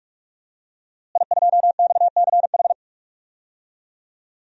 Morse code, heard as a keyed tone, about 43 wpm, 700 Hz. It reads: I2XCH
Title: DXCC-100 Call Signs - 5 Characters - Repeated 3x - 45wpm